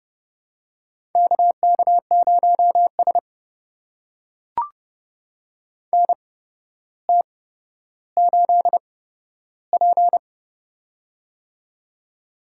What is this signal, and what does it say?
Morse code, heard as a keyed tone, about 32 wpm, spaced as for 13 wpm, 700 Hz. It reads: KK0S NT8P
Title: USA Call Signs - 4 Characters - 8x Character Spacing - 30wpm